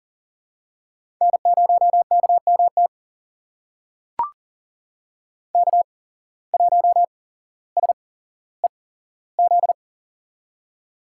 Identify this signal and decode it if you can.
Morse code, heard as a keyed tone, about 40 wpm, spaced as for 17 wpm, 700 Hz. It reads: N0KMT K1SEZ
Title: USA Call Signs - 5 Characters - 8x Character Spacing - 40wpm